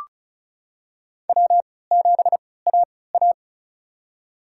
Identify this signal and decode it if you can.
Morse code, heard as a keyed tone, about 34 wpm, spaced as for 20 wpm, 700 Hz. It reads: W7AA
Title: USA Call Signs - 4 Characters - 3x Character Spacing - 35wpm